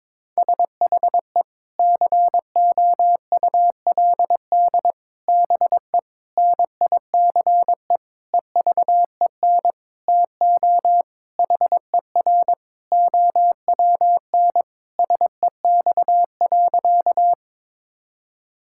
Morse code, heard as a keyed tone, about 22 wpm, 700 Hz. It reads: SHE COULD BE NICE EVEN TO HER OWN SEX.
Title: Sentences from Top 1600 Words 22wpm